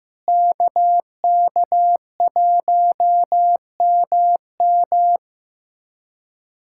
Morse code, heard as a keyed tone, about 15 wpm, 700 Hz. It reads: KK1MM
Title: K1USN SST QSO Practice - 15wpm